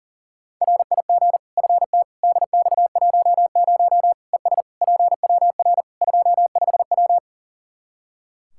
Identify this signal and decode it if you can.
Morse code, heard as a keyed tone, about 40 wpm, 700 Hz. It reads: RIG FT DX10 ES PWR 25W
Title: Practice QSO - Exchange 2 - Info - 40wpm